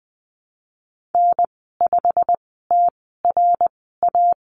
Morse code, heard as a keyed tone, about 20 wpm, spaced as for 15 wpm, 700 Hz. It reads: N5TRA
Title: 088 Lesson 15g - Number 5 - Call Signs 20wpm